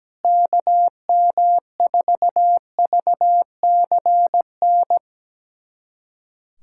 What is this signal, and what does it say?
Morse code, heard as a keyed tone, about 17 wpm, 700 Hz. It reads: KM4VCN
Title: USA Callsign DE Callsign - 6 Characters - 17wpm